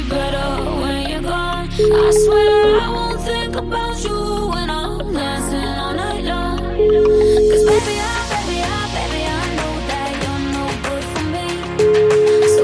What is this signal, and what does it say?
Ringback tone (Europe, 425 Hz), with noise.